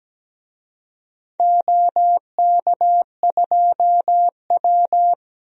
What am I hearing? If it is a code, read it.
Morse code, heard as a keyed tone, about 17 wpm, 700 Hz. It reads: OK2W
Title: DXCC-100 Call Signs - 4 Characters - 17wpm